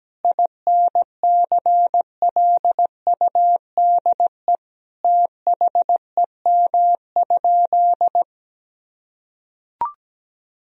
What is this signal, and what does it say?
Morse code, heard as a keyed tone, about 17 wpm, 700 Hz. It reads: INCLUDE THEM?
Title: Sentences from Top 1000 Words - Review Part 3 - 17wpm